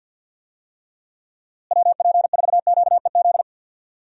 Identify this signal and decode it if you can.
Morse code, heard as a keyed tone, about 47 wpm, 700 Hz. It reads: WP4QEZ
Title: USA Call Signs - 6 Characters - Repeated 2x - 50wpm